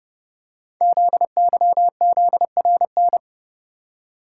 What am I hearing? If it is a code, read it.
Morse code, heard as a keyed tone, about 30 wpm, 700 Hz. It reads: ZYZRN
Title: Code Groups - 5 Letters - 30wpm